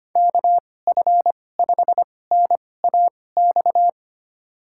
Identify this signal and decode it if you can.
Morse code, heard as a keyed tone, about 25 wpm, spaced as for 18 wpm, 700 Hz. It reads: KF5NAX
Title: USA Call Signs - 6 Characters - 2x Character Spacing - 25wpm